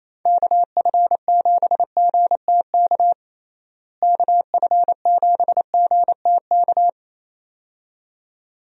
Morse code, heard as a keyed tone, about 27 wpm, 700 Hz. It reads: KF7GTK KF7GTK
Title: USA Call Signs - 6 Characters - Repeated 2x - 28wpm